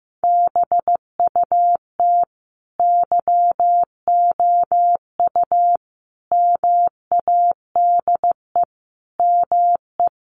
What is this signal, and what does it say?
Morse code, heard as a keyed tone, about 15 wpm, 700 Hz. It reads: BUT YOU MADE ME
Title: Sentences from Top 1300 Words 15wpm